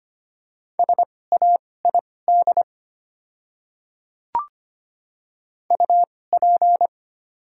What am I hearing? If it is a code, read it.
Morse code, heard as a keyed tone, about 25 wpm, spaced as for 18 wpm, 690 Hz. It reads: SAID UP
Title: 2x Character Spacing - Top 200 Words - 25wpm